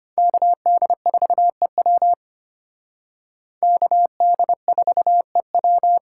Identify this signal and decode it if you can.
Morse code, heard as a keyed tone, about 32 wpm, 700 Hz. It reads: KD4EW KD4EW